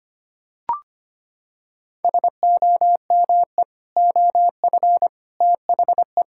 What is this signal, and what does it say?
Morse code, heard as a keyed tone, about 25 wpm, 700 Hz. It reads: SOME OF THE
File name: practice / 25wpm / Sentences with 3-Word Phrases - 25wpm